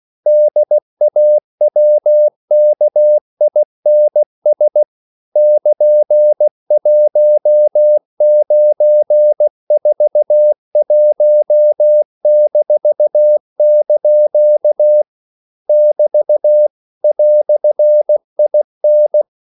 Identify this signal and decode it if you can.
Morse code, heard as a keyed tone, about 16 wpm, 600 Hz. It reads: DAWKINS <KN>1941-) <BT> "IN
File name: CW / Practice / EnglishQuotations0116